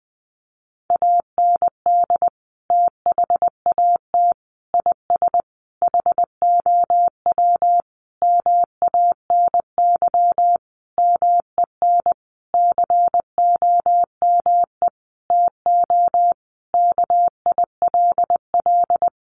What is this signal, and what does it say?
Morse code, heard as a keyed tone, about 20 wpm, 700 Hz. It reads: AND THAT IS HOW MANY MEN COME TO KILL
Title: Sentences from Top 1400 Words 20wpm